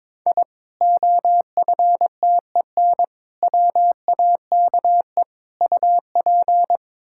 Morse code, heard as a keyed tone, about 22 wpm, 700 Hz. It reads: I OFTEN WAKE UP
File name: practice / 22wpm / Sentences with 8-Word Phrases - 22wpm